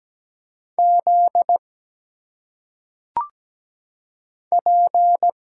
Morse code, heard as a keyed tone, about 17 wpm, 700 Hz. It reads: Z P
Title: Single Letters - Phonetic Pronunciation 17wpm